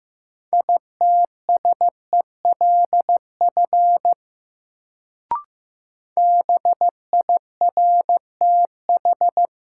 Morse code, heard as a keyed tone, about 15 wpm, 700 Hz. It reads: ITSELF BIRTH
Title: Sentences from Top 800 Words - New Word Drill - 15wpm